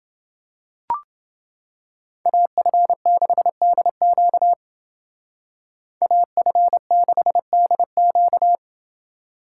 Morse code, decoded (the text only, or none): AF6DQ AF6DQ